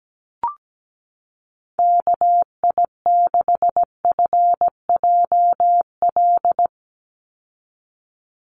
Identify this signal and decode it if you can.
Morse code, heard as a keyed tone, about 17 wpm, 700 Hz. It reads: KI6FJL
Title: USA Call Signs 17wpm